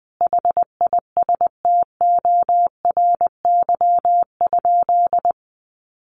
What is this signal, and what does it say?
Morse code, heard as a keyed tone, about 20 wpm, 700 Hz. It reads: HISTORY?
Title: Sentences from Top 1500 Words 20wpm